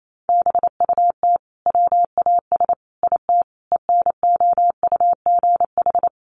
Morse code, heard as a keyed tone, about 28 wpm, 700 Hz. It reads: BUT WAS IT ENOUGH
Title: Sentences from Top 1400 Words 28wpm